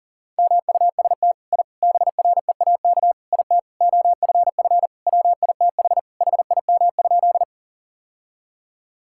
Morse code, heard as a keyed tone, about 37 wpm, 700 Hz. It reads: MUST I BREAK IT OFF WITH HIM?
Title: Sentences from Top 1000 Words - Review Part 6 - 40wpm